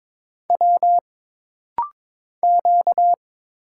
Morse code, heard as a keyed tone, about 22 wpm, 700 Hz. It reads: W Q
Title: Single Letter-Number Alt - Rapid-Fire 22wpm